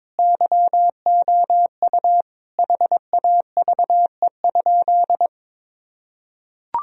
Morse code, heard as a keyed tone, about 22 wpm, 700 Hz. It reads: YOU HAVE?
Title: Sentences from Top 1000 Words - Review Part 3 - 22wpm